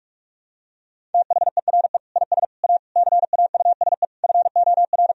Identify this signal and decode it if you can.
Morse code, heard as a keyed tone, about 41 wpm, 700 Hz. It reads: THERE IS A CAUSE FOR
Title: Sentences from Top 1000 Words - Review Part 1 - 45wpm